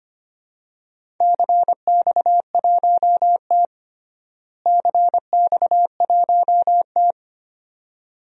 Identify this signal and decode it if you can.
Morse code, heard as a keyed tone, about 25 wpm, 700 Hz. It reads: CX1T CX1T
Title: DXCC-100 Call Signs - 4 Characters - Repeated 2x - 25wpm